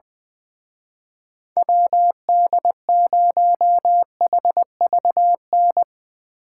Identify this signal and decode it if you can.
Morse code, heard as a keyed tone, about 22 wpm, 700 Hz. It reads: WD0HVN
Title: Speed Racing - USA Call Signs 20wpm